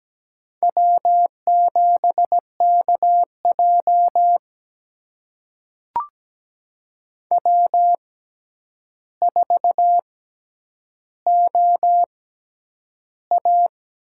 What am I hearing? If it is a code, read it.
Morse code, heard as a keyed tone, about 17 wpm, spaced as for 9 wpm, 700 Hz. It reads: W7KJ W4OA
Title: USA Call Signs - 4 Characters - 6x Character Spacing - 17wpm